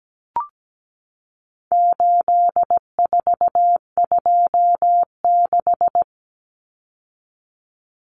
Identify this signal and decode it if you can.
Morse code, heard as a keyed tone, about 17 wpm, 700 Hz. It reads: 8426
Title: Numbers - 4-Digit 17wpm